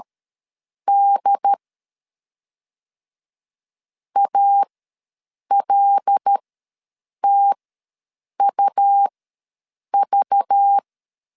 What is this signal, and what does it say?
Morse code, heard as a keyed tone, about 13 wpm, spaced as for 7 wpm, 800 Hz. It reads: D ALTUV
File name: LEZIONI / 5 LBVC / 5 LBV ws 12wpm 3015